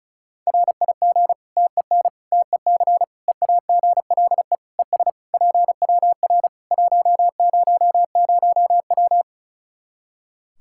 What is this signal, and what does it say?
Morse code, heard as a keyed tone, about 34 wpm, 700 Hz. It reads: RIG TEN TEC EAGLE ES PWR 100W
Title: Practice QSO - Exchange 2 - Info - 35wpm